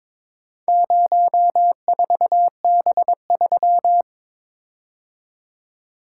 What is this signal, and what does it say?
Morse code, heard as a keyed tone, about 22 wpm, 700 Hz. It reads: 04B3
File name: practice / 22wpm / Four Letter-Number Combinations - 22wpm